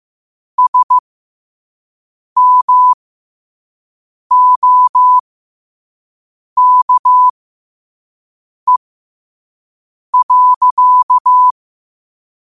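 Morse code, heard as a keyed tone, about 15 wpm, spaced as for 5 wpm, 1000 Hz. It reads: SMOKE.
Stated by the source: Morse quote of the day at 5 wpm.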